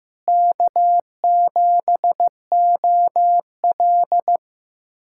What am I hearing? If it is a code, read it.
Morse code, heard as a keyed tone, about 15 wpm, 700 Hz. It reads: K7OL
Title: USA Callsign DE Callsign - 4 Characters - 15wpm